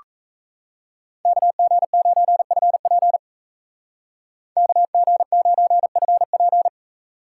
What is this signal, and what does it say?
Morse code, heard as a keyed tone, about 33 wpm, 700 Hz. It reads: KG9FP KG9FP
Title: Speed Racing - USA Call Signs 28wpm